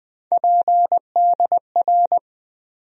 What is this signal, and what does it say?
Morse code, heard as a keyed tone, about 20 wpm, 700 Hz. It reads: PDR
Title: USA Call Sign Postfixes - 3 Letters 20wpm